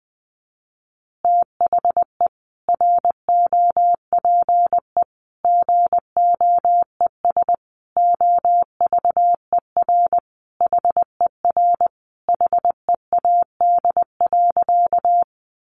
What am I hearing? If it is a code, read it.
Morse code, heard as a keyed tone, about 20 wpm, 700 Hz. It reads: THE ROPE GOES OVER HER HEAD.
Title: Easy Phrases 20wpm